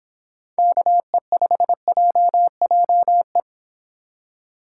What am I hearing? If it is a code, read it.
Morse code, heard as a keyed tone, about 29 wpm, 700 Hz. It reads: KE5JJE